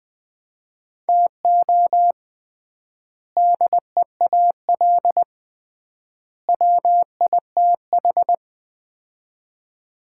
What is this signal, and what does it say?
Morse code, heard as a keyed tone, about 20 wpm, 700 Hz. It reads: TO DEAL WITH
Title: 3x Word-Spacing - Sets of 3 Words - 20wpm